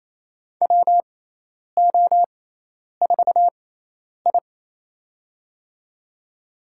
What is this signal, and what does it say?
Morse code, heard as a keyed tone, about 28 wpm, spaced as for 10 wpm, 700 Hz. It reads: WO4I
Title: USA Call Signs - 6x Character Spacing - 28wpm